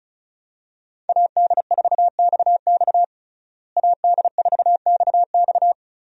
Morse code, heard as a keyed tone, about 35 wpm, 700 Hz. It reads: AD4XX AD4XX
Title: USA Call Signs - 5 Characters - Repeated 3x - 35wpm